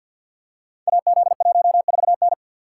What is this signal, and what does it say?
Morse code, heard as a keyed tone, about 49 wpm, 700 Hz. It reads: AZ14N
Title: Code Groups - 5 Letter:Numbers - 50wpm